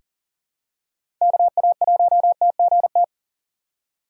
Morse code, heard as a keyed tone, about 40 wpm, 700 Hz. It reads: KA1TGT
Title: USA Call Signs - 6 Characters - 2x Character Spacing - 40wpm